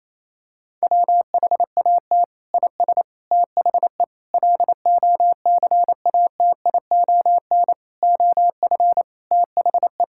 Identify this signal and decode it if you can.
Morse code, heard as a keyed tone, about 28 wpm, 700 Hz. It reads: WHAT IS THE LOCATION OF THE
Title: Sentences from Top 1000 Words - Review Part 6 - 28wpm